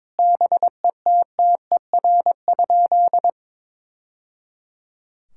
Morse code, heard as a keyed tone, about 22 wpm, 700 Hz. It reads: BETTER?
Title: Sentences from Top 1000 Words - Review Part 6 - 22wpm